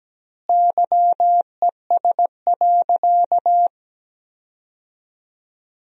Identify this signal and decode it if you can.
Morse code, heard as a keyed tone, about 17 wpm, 700 Hz. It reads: YES.
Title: 2x Word-Spacing - Sentences from Top 500 Words - 17wpm